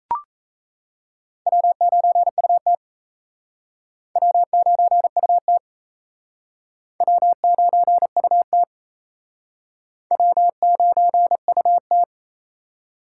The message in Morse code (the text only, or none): W9UT W9UT W9UT W9UT